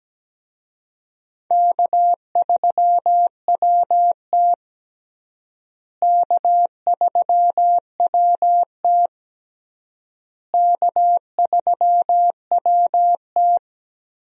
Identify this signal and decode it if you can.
Morse code, heard as a keyed tone, about 17 wpm, 700 Hz. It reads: K3WT K3WT K3WT
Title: USA Call Signs - 4 Characters - Repeated 3x - 17wpm